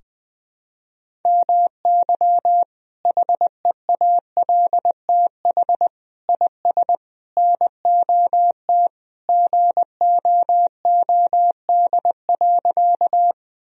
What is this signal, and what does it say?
Morse code, heard as a keyed tone, about 21 wpm, 700 Hz. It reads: MY HEALTH IS NOT GOOD.
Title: Sentences from Top 1000 Words - Review Part 3 - 20wpm